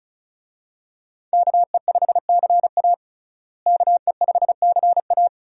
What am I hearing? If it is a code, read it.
Morse code, heard as a keyed tone, about 36 wpm, 700 Hz. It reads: KE5CA KE5CA
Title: USA Call Signs - 5 Characters - Repeated 2x - 35wpm